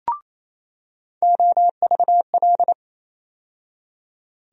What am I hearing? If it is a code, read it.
Morse code, heard as a keyed tone, about 28 wpm, 700 Hz. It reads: OVL
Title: USA Call Sign Postfixes - 3 Letters 28wpm